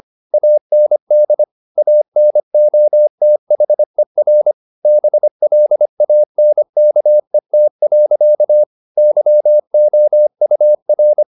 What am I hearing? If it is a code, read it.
Morse code, heard as a keyed tone, about 25 wpm, 600 Hz. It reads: AND ANOTHER BLANKET. YOUR
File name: the old man and the sea part 1 25wpm